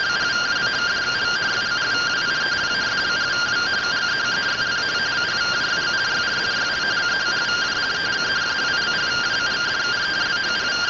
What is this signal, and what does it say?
Two-tone teleprinter signal, about 170 Hz shift.